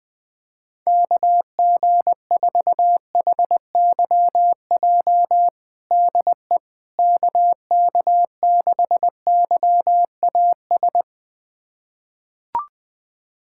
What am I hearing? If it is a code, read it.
Morse code, heard as a keyed tone, about 20 wpm, 700 Hz. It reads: KG4HYJ DE KK6YAS
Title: USA Callsign DE Callsign - 6 Characters - 20wpm